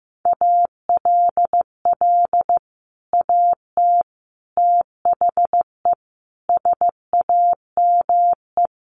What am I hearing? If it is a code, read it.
Morse code, heard as a keyed tone, about 15 wpm, 700 Hz. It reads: ALL AT THE SAME
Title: Sets of 5 Words Encore2 15wpm